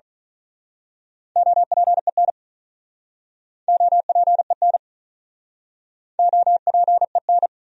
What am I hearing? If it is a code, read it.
Morse code, heard as a keyed tone, about 43 wpm, 700 Hz. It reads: OPEN OPEN OPEN
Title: Speed-Racing Top 500 Words 35wpm